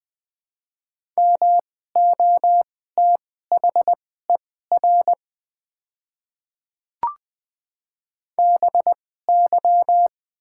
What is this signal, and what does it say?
Morse code, heard as a keyed tone, about 20 wpm, spaced as for 14 wpm, 700 Hz. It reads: MOTHER BY
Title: 2x Character Spacing - Top 200 Words - 20wpm